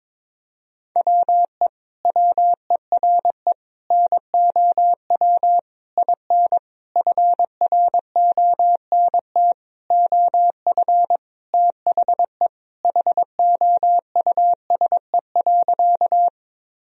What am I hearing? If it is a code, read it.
Morse code, heard as a keyed tone, about 22 wpm, 700 Hz. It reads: WE WERE NOW IN FRONT OF THE HOUSE.
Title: Sentences from Top 1000 Words - Review Part 2 - 22wpm